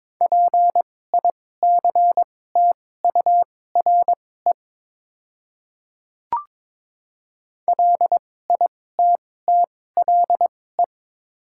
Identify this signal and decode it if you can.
Morse code, heard as a keyed tone, about 22 wpm, spaced as for 16 wpm, 700 Hz. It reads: PICTURE LITTLE
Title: 2x Character Spacing - Top 200 Words - 22wpm